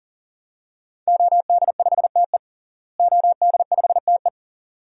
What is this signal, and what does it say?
Morse code, heard as a keyed tone, about 40 wpm, 700 Hz. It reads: OD5TE OD5TE
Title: DXCC-100 Call Signs - 5 Characters - Repeated 2x - 40wpm